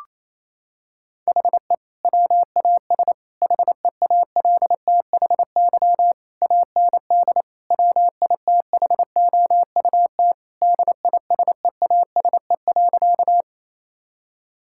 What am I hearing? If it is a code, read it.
Morse code, heard as a keyed tone, about 28 wpm, 700 Hz. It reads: HE WAS HEALTHY AND WITHOUT DISEASE.
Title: Sentences from Top 1000 Words - Review Part 6 - 28wpm